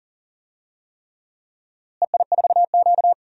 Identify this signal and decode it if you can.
Morse code, heard as a keyed tone, about 40 wpm, 700 Hz. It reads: EI4Q
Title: DXCC-100 Call Signs - 4 Characters - Repeated 3x - 40wpm